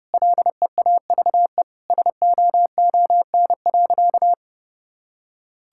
Morse code, heard as a keyed tone, about 32 wpm, 700 Hz. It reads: LEAVE SOON.
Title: Sentences from Top 1000 Words - Review Part 5 - 30wpm